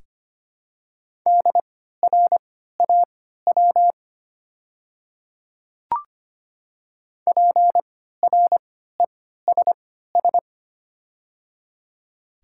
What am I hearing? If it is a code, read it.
Morse code, heard as a keyed tone, about 25 wpm, spaced as for 14 wpm, 700 Hz. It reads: DRAW PRESS
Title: 3x Character Spacing - Top 300 Words - 25wpm